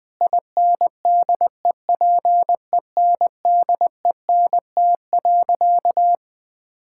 Morse code, heard as a keyed tone, about 20 wpm, 700 Hz. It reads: INDEPENDENT.
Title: Sentences from Top 1000 Words - Review Part 6 - 20wpm